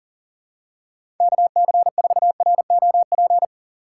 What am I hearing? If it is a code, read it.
Morse code, heard as a keyed tone, about 40 wpm, 700 Hz. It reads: KC4ROP
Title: USA Call Signs - 2x Character Spacing - 40wpm